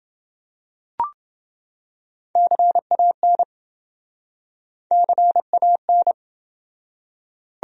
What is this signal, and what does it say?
Morse code, heard as a keyed tone, about 28 wpm, 700 Hz. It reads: CAN CAN